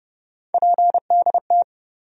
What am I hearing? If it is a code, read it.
Morse code, heard as a keyed tone, about 30 wpm, 700 Hz. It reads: PDT
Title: USA Call Sign Postfixes - 3 Letters 30wpm